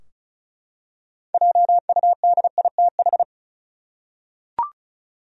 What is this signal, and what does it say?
Morse code, heard as a keyed tone, about 34 wpm, 700 Hz. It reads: JUDITH
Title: USA Common Names - 35wpm